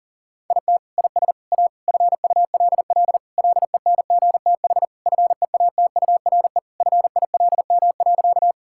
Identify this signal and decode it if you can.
Morse code, heard as a keyed tone, about 40 wpm, 700 Hz. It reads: IT IS A FULL LENGTH FEATURE FILM.
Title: Sentences from Top 1000 Words - Review Part 5 - 40wpm